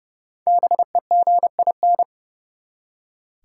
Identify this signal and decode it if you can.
Morse code, heard as a keyed tone, about 31 wpm, 700 Hz. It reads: BEGIN
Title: Five Letter Words - 30wpm